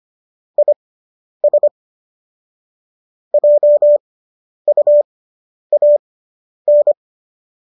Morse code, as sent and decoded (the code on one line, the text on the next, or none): .. ... / .--- ..- .- -.
IS JUAN